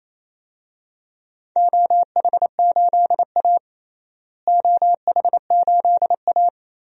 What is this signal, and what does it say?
Morse code, heard as a keyed tone, about 28 wpm, 700 Hz. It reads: OH8A OH8A
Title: DXCC-100 Call Signs - 4 Characters - Repeated 2x - 28wpm